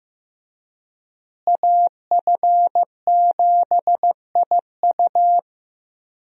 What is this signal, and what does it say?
Morse code, heard as a keyed tone, about 15 wpm, 700 Hz. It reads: AF7IU
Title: USA Call Signs - 5 Characters - Repeated 2x - 15wpm